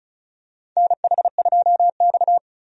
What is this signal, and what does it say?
Morse code, heard as a keyed tone, about 36 wpm, 700 Hz. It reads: NH2X
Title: USA Call Signs - 4 Characters - 35wpm